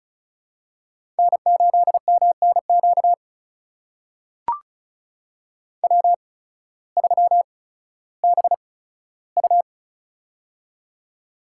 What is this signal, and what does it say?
Morse code, heard as a keyed tone, about 36 wpm, spaced as for 16 wpm, 700 Hz. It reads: N8MNQ W3BU
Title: USA Call Signs - 8x Character Spacing - 35wpm